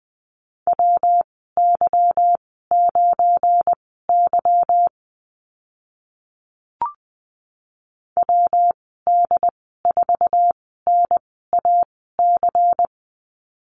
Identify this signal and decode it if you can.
Morse code, heard as a keyed tone, about 20 wpm, spaced as for 15 wpm, 700 Hz. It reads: WY9Y WD4NAC
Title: 179 Lesson 29g - Slash - Call Signs 20wpm